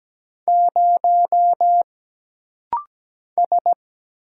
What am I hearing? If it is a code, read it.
Morse code, heard as a keyed tone, about 17 wpm, 700 Hz. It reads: 0 S
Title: Single Letter-Number - Rapid-Fire 17wpm